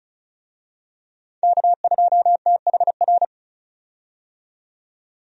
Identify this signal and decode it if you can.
Morse code, heard as a keyed tone, about 34 wpm, 700 Hz. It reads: K2THR
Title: USA Call Signs - 5 Characters - 35wpm